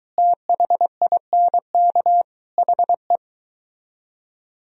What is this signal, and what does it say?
Morse code, heard as a keyed tone, about 23 wpm, 700 Hz. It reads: THINK HE